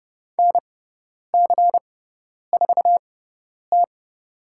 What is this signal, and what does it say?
Morse code, heard as a keyed tone, about 30 wpm, spaced as for 10 wpm, 700 Hz. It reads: NC4T
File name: Lesson 16g - Letter c - Call Signs 30wpm